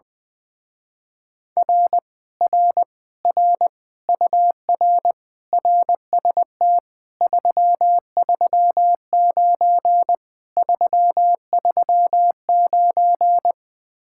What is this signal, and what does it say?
Morse code, heard as a keyed tone, about 20 wpm, 700 Hz. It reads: R R R UR RST 339 339
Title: Speed-Racing Common QSO Phrases Encore - 20wpm